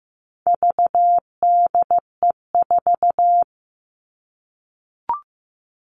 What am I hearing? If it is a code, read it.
Morse code, heard as a keyed tone, about 15 wpm, 700 Hz. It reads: VDE4
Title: All Cut Numbers - 4-Digit 15wpm